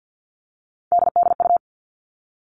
Morse code, heard as a keyed tone, about 50 wpm, 700 Hz. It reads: DDU
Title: All Cut Numbers - 3-Digit 50wpm